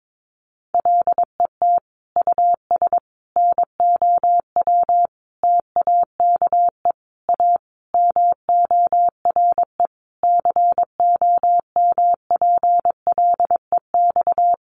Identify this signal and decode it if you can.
Morse code, heard as a keyed tone, about 22 wpm, 700 Hz. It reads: LET US NOW TAKE A MORE COMPLEX
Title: Sentences from Top 900 Words 22wpm